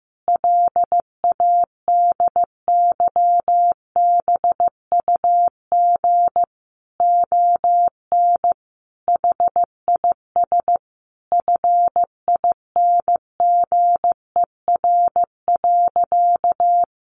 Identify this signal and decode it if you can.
Morse code, heard as a keyed tone, about 15 wpm, 700 Hz. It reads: LADYBUG ON HIS FINGER.
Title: Additional Easy Phrases 15wpm